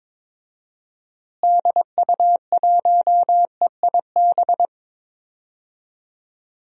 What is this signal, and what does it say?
Morse code, heard as a keyed tone, about 22 wpm, 700 Hz. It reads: DU1EIB
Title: DXCC-100 Call Signs - 6 Characters - Repeated 3x - 22wpm